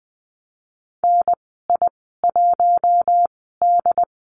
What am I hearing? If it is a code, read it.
Morse code, heard as a keyed tone, about 20 wpm, spaced as for 15 wpm, 700 Hz. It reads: NI1D
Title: 076 Lesson 13g - Letter L - Call Signs 20wpm